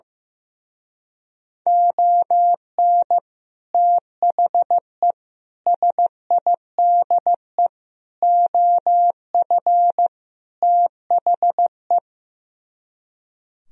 Morse code, heard as a keyed tone, about 16 wpm, 700 Hz. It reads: ON THE SIDE OF THE
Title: Speed-Racing - Sets of 5 Words 15wpm